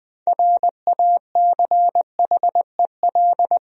Morse code, heard as a keyed tone, about 20 wpm, 700 Hz. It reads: RACHEL
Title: USA Common Names - 20wpm